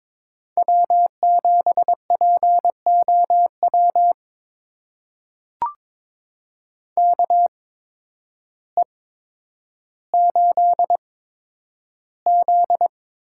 Morse code, heard as a keyed tone, about 22 wpm, spaced as for 10 wpm, 700 Hz. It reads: W7POW KE8Z
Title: USA Call Signs - 8x Character Spacing - 22wpm